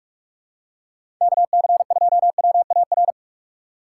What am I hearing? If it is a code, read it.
Morse code, heard as a keyed tone, about 46 wpm, 700 Hz. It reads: KC2WAR
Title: USA Call Signs - 2x Character Spacing - 45wpm